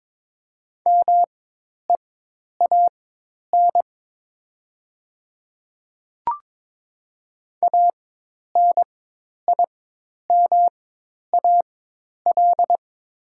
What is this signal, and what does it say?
Morse code, heard as a keyed tone, about 22 wpm, spaced as for 10 wpm, 700 Hz. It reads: MEAN ANIMAL
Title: 4x Character Spacing - Top 500 Words - 22wpm